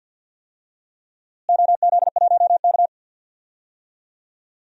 Morse code, heard as a keyed tone, about 49 wpm, 700 Hz. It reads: OZ1K
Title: DXCC-100 Call Signs - 4 Characters - Repeated 2x - 50wpm